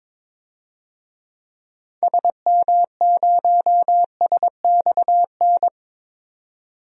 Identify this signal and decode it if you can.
Morse code, heard as a keyed tone, about 21 wpm, 700 Hz. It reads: SM0SXN
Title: DXCC-100 Call Signs - 6 Characters - 22wpm